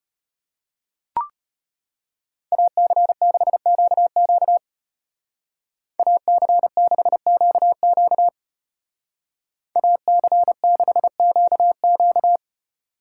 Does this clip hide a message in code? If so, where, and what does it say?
Yes: Morse code from 2.5 to 12.4 s, heard as a keyed tone, about 34 wpm, 700 Hz. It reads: AC6QQ AC6QQ AC6QQ